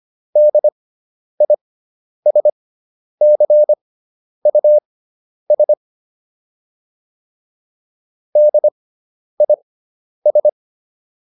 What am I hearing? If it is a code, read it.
Morse code, heard as a keyed tone, about 25 wpm, spaced as for 10 wpm, 600 Hz. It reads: DISCUS DIS